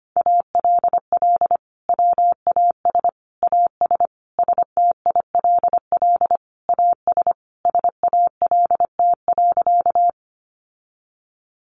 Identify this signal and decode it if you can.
Morse code, heard as a keyed tone, about 25 wpm, 700 Hz. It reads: ALL WAS AS STILL AS SALT.
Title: Sentences from Top 800 Words 25wpm